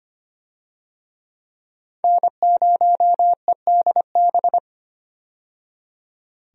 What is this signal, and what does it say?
Morse code, heard as a keyed tone, about 25 wpm, 700 Hz. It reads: N0EDB
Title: USA Call Signs - 5 Characters - 25wpm